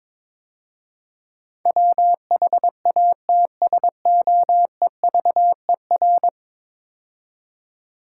Morse code, heard as a keyed tone, about 22 wpm, 700 Hz. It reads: WHATSOEVER
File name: practice / 22wpm / 10 Letter Words - First 3 Letters Given - 22wpm